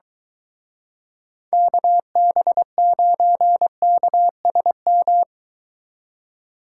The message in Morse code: KB9KSM

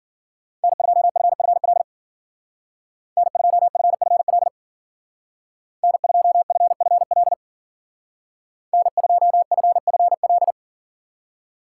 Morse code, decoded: N2FFL N2FFL N2FFL N2FFL

